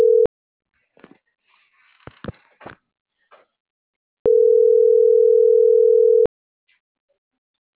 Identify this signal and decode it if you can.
Ringback tone (North America), 440 + 480 Hz.